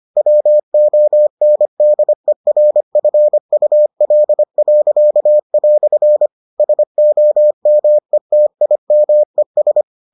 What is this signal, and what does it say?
Morse code, heard as a keyed tone, about 25 wpm, 600 Hz. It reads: WONDERFUL." SOMETIMES